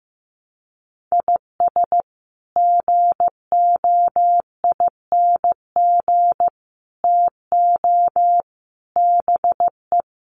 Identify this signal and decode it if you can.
Morse code, heard as a keyed tone, about 15 wpm, 700 Hz. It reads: IS GOING TO BE
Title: Sets of 4 Words 15wpm